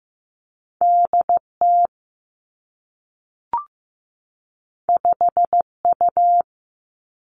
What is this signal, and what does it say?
Morse code, heard as a keyed tone, about 15 wpm, 700 Hz. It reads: D T 5 U
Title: Two Challenging Letter-Number Combinations 15wpm